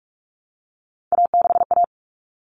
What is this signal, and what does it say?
Morse code, heard as a keyed tone, about 44 wpm, 700 Hz. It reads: A6A
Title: Common Cut Numbers - 3-Digit 45wpm